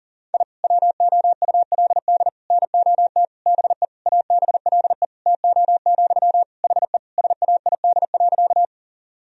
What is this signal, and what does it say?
Morse code, heard as a keyed tone, about 40 wpm, 700 Hz. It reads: I WOULD NOT BE ABLE TO, HE SAID.
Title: Sentences from Top 1000 Words - Review Part 2 - 40wpm